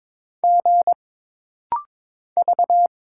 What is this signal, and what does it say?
Morse code, heard as a keyed tone, about 22 wpm, 700 Hz. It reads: G V
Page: This rapid-fire timing is geared to push your ICR (Instant Character Recognition) skills.